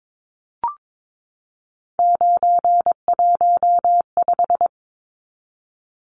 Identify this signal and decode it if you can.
Morse code, heard as a keyed tone, about 22 wpm, 700 Hz. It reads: 915
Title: Numbers - 3-Digit 22wpm